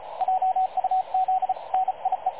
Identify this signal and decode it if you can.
Morse code, heard as a keyed tone, about 35 wpm, 750 Hz, with radio noise. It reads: JUZ NIE
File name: dlazaawansowanych / nagraniazpasma / websdr recording 3567.0kHz